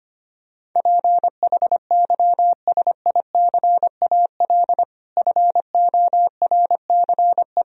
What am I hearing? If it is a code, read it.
Morse code, heard as a keyed tone, about 25 wpm, 700 Hz. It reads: PHYSICAL FORCE
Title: Sentences from Top 1000 Words - Review Part 4 - 25wpm